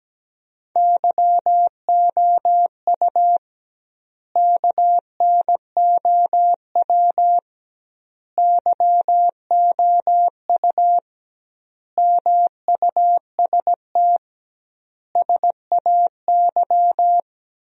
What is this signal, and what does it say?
Morse code, heard as a keyed tone, about 17 wpm, 700 Hz. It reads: YOU KNOW YOU MUST SAY
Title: 2x Word-Spacing - Sentences from Top 500 Words - 17wpm